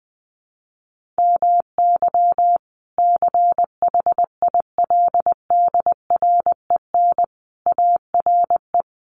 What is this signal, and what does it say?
Morse code, heard as a keyed tone, about 20 wpm, 700 Hz. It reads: MY CHILDREN ARE
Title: Sentences from Top 1500 Words 20wpm